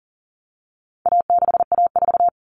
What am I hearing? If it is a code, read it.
Morse code, heard as a keyed tone, about 40 wpm, 700 Hz. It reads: A6A4
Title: Common Cut Numbers - 4-Digit 40wpm